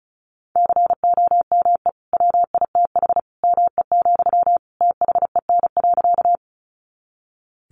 Morse code, heard as a keyed tone, about 34 wpm, 700 Hz. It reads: COME WITH ME, THEN.
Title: Sentences from Top 200 Words 35wpm